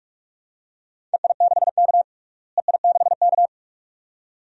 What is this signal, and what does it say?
Morse code, heard as a keyed tone, about 46 wpm, 700 Hz. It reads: EI6K EI6K
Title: DXCC-100 Call Signs - 4 Characters - Repeated 2x - 45wpm